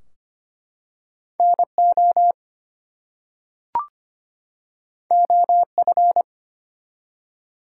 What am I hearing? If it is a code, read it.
Morse code, heard as a keyed tone, about 25 wpm, 700 Hz. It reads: NO OF
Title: Two Letter Words - 25wpm